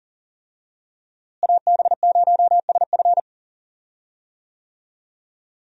Morse code, heard as a keyed tone, about 40 wpm, 700 Hz. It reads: AB0SF
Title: USA Call Signs - 5 Characters - Repeated 3x - 40wpm